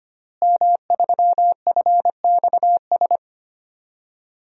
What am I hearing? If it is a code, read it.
Morse code, heard as a keyed tone, about 25 wpm, 700 Hz. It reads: M3FXS
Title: DXCC-100 Call Signs - 5 Characters - Repeated 2x - 25wpm